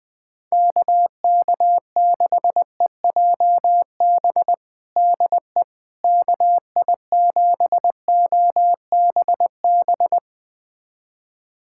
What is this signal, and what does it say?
Morse code, heard as a keyed tone, about 20 wpm, 700 Hz. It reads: KK6EJB DE KI7OBB
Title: USA Callsign DE Callsign - 6 Characters - 20wpm